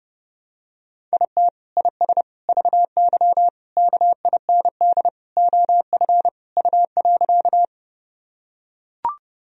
Morse code, heard as a keyed tone, about 30 wpm, 700 Hz. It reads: IT IS VY KIND OF U.
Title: Sentences from Top 200 Words and Abbreviations 30wpm